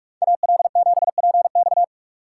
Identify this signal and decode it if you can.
Morse code, heard as a keyed tone, about 44 wpm, 700 Hz. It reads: AL7PX
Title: USA Call Signs - 5 Characters - 6x Character Spacing - 45wpm